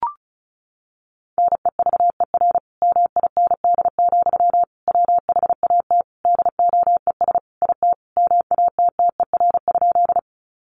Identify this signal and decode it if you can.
Morse code, heard as a keyed tone, about 34 wpm, 700 Hz. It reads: NEVER MIND, WHAT DOES IT MATTER?
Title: Sentences from Top 700 Words 35wpm